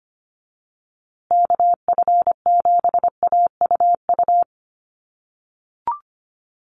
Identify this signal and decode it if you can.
Morse code, heard as a keyed tone, about 25 wpm, 700 Hz. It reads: KF7AUU
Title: USA Call Signs Repeated 3x 25wpm